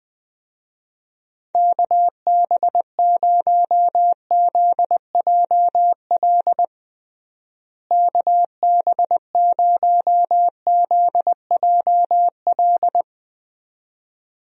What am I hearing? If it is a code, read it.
Morse code, heard as a keyed tone, about 20 wpm, 700 Hz. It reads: KB0ZJL KB0ZJL
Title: USA Call Signs - 6 Characters - Repeated 2x - 20wpm